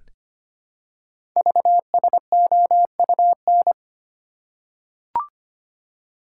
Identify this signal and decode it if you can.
Morse code, heard as a keyed tone, about 25 wpm, 700 Hz. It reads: VSOUN
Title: Code Groups - 5 Letters - 25wpm